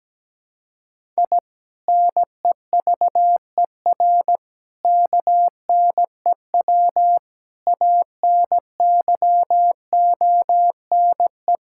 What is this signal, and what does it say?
Morse code, heard as a keyed tone, about 17 wpm, 700 Hz. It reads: I NEVER KNEW ANYONE
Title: Sentences from Top 1000 Words - Review Part 6 - 17wpm